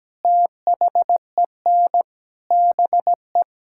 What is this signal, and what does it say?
Morse code, heard as a keyed tone, about 17 wpm, 700 Hz. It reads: THEN BE
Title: Sentences from Top 1600 Words 17wpm